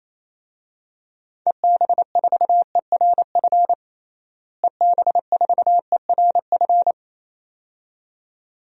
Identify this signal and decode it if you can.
Morse code, heard as a keyed tone, about 28 wpm, 700 Hz. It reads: EB4ERF EB4ERF
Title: DXCC-100 Call Signs - 6 Characters - Repeated 2x - 28wpm